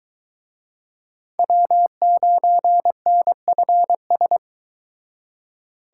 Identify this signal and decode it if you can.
Morse code, heard as a keyed tone, about 23 wpm, 700 Hz. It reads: W9NFS